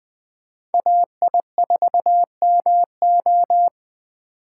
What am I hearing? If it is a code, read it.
Morse code, heard as a keyed tone, about 20 wpm, 700 Hz. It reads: AI4MO
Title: USA Call Signs - 5 Characters - 3x Character Spacing - 20wpm